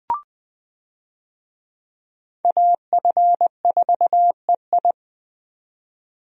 Morse code, heard as a keyed tone, about 20 wpm, 700 Hz. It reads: AF4EI
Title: USA Call Signs - 5 Characters - 20wpm